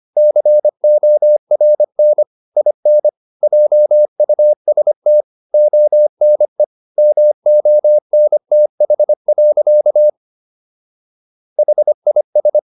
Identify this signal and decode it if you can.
Morse code, heard as a keyed tone, about 25 wpm, 600 Hz. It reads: CORN IN JUST ONE MONTH. HIS
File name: SS111 25